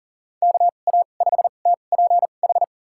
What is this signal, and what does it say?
Morse code, heard as a keyed tone, about 40 wpm, spaced as for 29 wpm, 700 Hz. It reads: KA5TPH
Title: USA Call Signs - 6 Characters - 2x Character Spacing - 40wpm